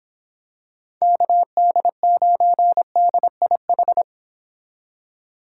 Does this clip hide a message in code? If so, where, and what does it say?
Yes: Morse code from 1.0 to 4.0 s, heard as a keyed tone, about 26 wpm, 700 Hz. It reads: KD9DIH